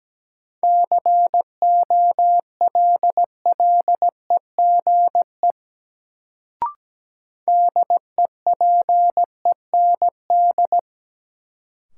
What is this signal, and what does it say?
Morse code, heard as a keyed tone, about 17 wpm, 700 Hz. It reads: COLLEGE DEPEND
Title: Sentences from Top 1300 Words - New Word Drill - 17wpm